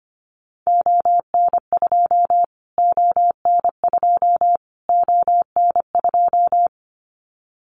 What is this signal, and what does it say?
Morse code, heard as a keyed tone, about 25 wpm, 700 Hz. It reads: ON2 ON2 ON2
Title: Callsign Prefixes for Top 25 DXCC - Repeated 3x - 25wpm